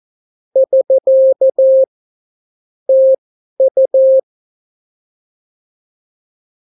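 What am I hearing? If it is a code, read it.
Morse code, heard as a keyed tone, about 14 wpm, spaced as for 11 wpm, 530 Hz. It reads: <SK> TU
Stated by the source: Example Spanish CW QSO from a Morse learning book, 14 wpm characters with 8 wpm spacing.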